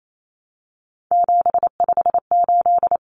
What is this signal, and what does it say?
Morse code, heard as a keyed tone, about 28 wpm, 700 Hz. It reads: 758
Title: Numbers - 3-Digit 28wpm